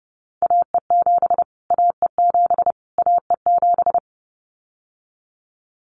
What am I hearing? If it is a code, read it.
Morse code, heard as a keyed tone, about 30 wpm, 700 Hz. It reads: AE7 AE7 AE7
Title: Callsign Prefixes for Top 25 DXCC - Repeated 3x - 30wpm